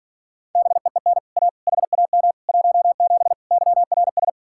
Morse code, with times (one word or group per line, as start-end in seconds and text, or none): 0.6-1.2 BEEN
1.4-1.5 A
1.7-2.3 HAM
2.5-3.3 17
3.5-4.3 YRS